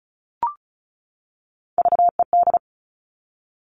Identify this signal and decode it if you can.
Morse code, heard as a keyed tone, about 34 wpm, 700 Hz. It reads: VED
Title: All Cut Numbers - 3-Digit 35wpm